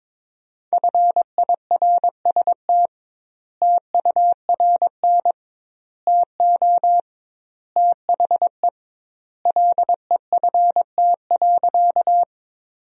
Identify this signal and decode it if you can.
Morse code, heard as a keyed tone, about 22 wpm, 700 Hz. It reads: FIRST TURN TO THE LEFT.
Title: 2x Word-Spacing - Sentences from Top 300 Words - 22wpm